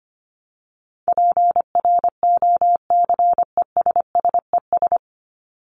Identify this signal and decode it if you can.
Morse code, heard as a keyed tone, about 25 wpm, 700 Hz. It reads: PROCESSES
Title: ES:S Suffix - Top 100 Words with Base Words 25wpm